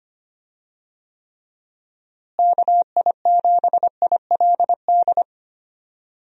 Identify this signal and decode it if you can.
Morse code, heard as a keyed tone, about 25 wpm, 700 Hz. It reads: KI7ILD
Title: USA Call Signs - 6 Characters - Repeated 3x - 25wpm